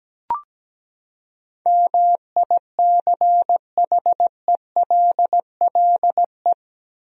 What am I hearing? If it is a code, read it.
Morse code, heard as a keyed tone, about 17 wpm, 700 Hz. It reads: MICHELLE
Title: USA Common Names - 17wpm